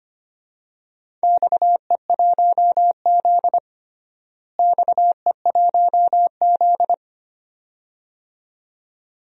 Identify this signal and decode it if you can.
Morse code, heard as a keyed tone, about 25 wpm, 700 Hz. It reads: XE1Z XE1Z
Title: DXCC-100 Call Signs - 4 Characters - Repeated 2x - 25wpm